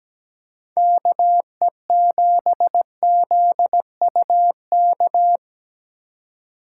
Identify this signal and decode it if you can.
Morse code, heard as a keyed tone, about 17 wpm, 700 Hz. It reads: KE7ZUK
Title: USA Call Signs - 6 Characters - 2x Character Spacing - 17wpm